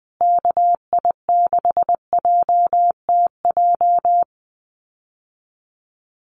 Morse code, heard as a keyed tone, about 20 wpm, 700 Hz. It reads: KI6JTJ
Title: USA Call Signs Repeated 2x 20wpm